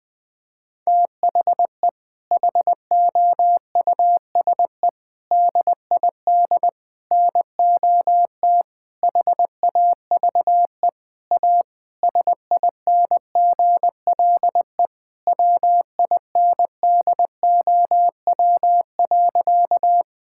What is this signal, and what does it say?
Morse code, heard as a keyed tone, about 20 wpm, 700 Hz. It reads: THE HOUSE DID NOT HAVE A SINGLE WINDOW.
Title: Sentences from Top 1600 Words 20wpm